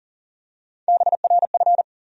Morse code, heard as a keyed tone, about 40 wpm, 700 Hz. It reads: BRF
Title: USA Call Sign Postfixes - 3 Letters 40wpm